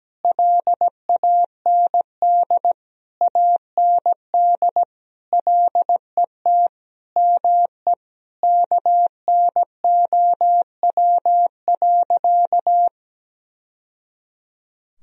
Morse code, heard as a keyed tone, about 17 wpm, 700 Hz. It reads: LAND AND LET ME KNOW.
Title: Sentences from Top 1000 Words - Review Part 2 - 17wpm